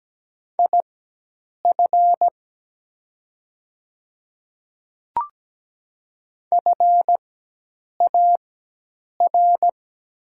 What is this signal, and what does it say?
Morse code, heard as a keyed tone, about 17 wpm, spaced as for 8 wpm, 700 Hz. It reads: IF FAR
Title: 4x Character Spacing - Top 300 Words - 17wpm